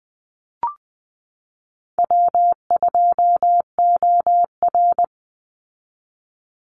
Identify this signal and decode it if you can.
Morse code, heard as a keyed tone, about 20 wpm, 700 Hz. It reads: W2OR
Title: Easy Call Signs - 20wpm